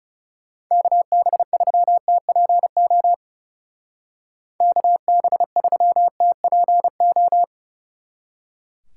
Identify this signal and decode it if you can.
Morse code, heard as a keyed tone, about 37 wpm, 700 Hz. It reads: KB3TPO KB3TPO
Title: Speed Racing - USA 2x3 Call Signs 30wpm